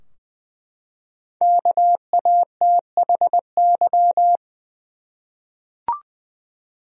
Morse code, heard as a keyed tone, about 20 wpm, 700 Hz. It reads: KATHY
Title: USA Common Names - Max 5 Letters - 20wpm